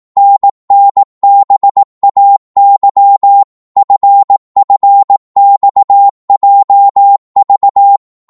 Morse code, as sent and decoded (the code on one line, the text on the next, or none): -. -. -... .- -.-- / ..-. ..-. -..- .--- ...-
NNBAY FFXJV